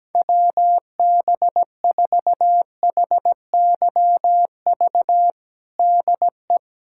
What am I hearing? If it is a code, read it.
Morse code, heard as a keyed tone, about 17 wpm, 700 Hz. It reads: WB4HYV DE
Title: USA Callsign DE Callsign - 6 Characters - 17wpm